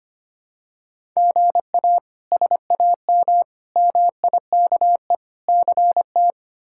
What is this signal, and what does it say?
Morse code, heard as a keyed tone, about 25 wpm, 700 Hz. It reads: GA SAM MIKE CT
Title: K1USN SST QSO Practice - 25wpm